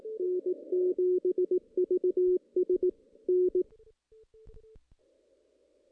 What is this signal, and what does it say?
Morse code, heard as a keyed tone, about 18 wpm, 370 Hz. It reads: N7VS N
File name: 2016 JAN CW NAQP / 2016 Jan naqp k5ot - 0-8